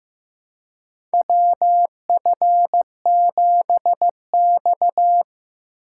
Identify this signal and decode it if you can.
Morse code, heard as a keyed tone, about 15 wpm, 700 Hz. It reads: WF7X
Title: USA Call Signs - 3x Character Spacing - 15wpm